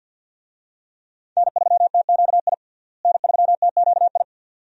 Morse code, heard as a keyed tone, about 49 wpm, 700 Hz. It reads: N3TQI N3TQI
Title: USA Call Signs - 5 Characters - Repeated 2x - 50wpm